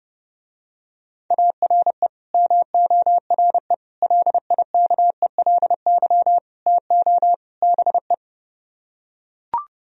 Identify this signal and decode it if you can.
Morse code, heard as a keyed tone, about 32 wpm, 700 Hz. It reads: ARE MORE LIKELY TO BE